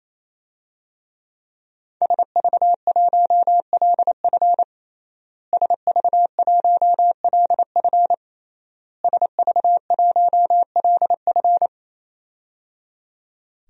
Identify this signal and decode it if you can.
Morse code, heard as a keyed tone, about 28 wpm, 700 Hz. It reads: SV1LF SV1LF SV1LF
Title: DXCC-100 Call Signs - 5 Characters - Repeated 3x - 28wpm